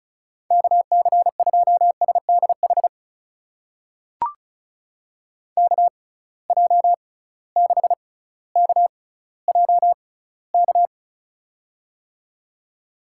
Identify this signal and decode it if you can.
Morse code, heard as a keyed tone, about 36 wpm, spaced as for 18 wpm, 700 Hz. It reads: KC2SDH KJ6KJK
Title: USA Call Signs - 6 Characters - 6x Character Spacing - 35wpm